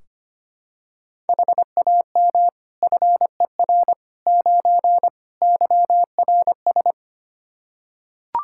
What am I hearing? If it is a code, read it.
Morse code, heard as a keyed tone, about 25 wpm, 700 Hz. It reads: HAM FER 9 YRS
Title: K1USN SST QSO Practice 25wpm